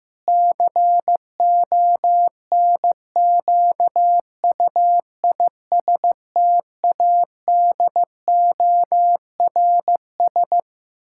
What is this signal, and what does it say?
Morse code, heard as a keyed tone, about 15 wpm, 700 Hz. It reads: CONQUISTADORS
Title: Pangram Challenge v2 - 15wpm